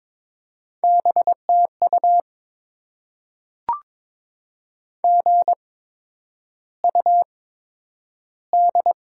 Morse code, heard as a keyed tone, about 22 wpm, spaced as for 9 wpm, 700 Hz. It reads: BTU GUD
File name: Top 20 QSO Elements - 8x Character Spacing - 22wpm